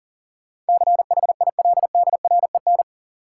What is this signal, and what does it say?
Morse code, heard as a keyed tone, about 37 wpm, 700 Hz. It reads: CHILDREN
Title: Sentences from Top 300 Words - New Word Drill - 40wpm